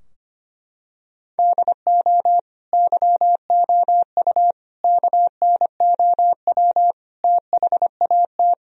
Morse code, heard as a keyed tone, about 25 wpm, 700 Hz. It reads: DO YOU KNOW THAT
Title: Sentences from Top 1000 Words - Review Part 3 - 25wpm